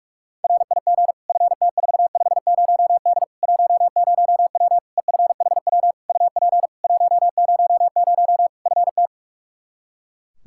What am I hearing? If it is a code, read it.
Morse code, heard as a keyed tone, about 46 wpm, 700 Hz. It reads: RIG FT450D 10W EFHW UP 100 FT
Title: K1USN SST QSO Practice 45wpm